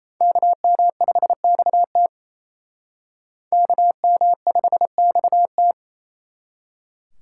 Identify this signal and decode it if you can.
Morse code, heard as a keyed tone, about 35 wpm, 700 Hz. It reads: KM5XT KM5XT